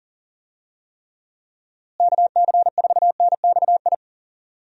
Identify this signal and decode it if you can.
Morse code, heard as a keyed tone, about 40 wpm, 700 Hz. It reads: KC4NXI
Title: USA Call Signs - 6 Characters - Repeated 2x - 40wpm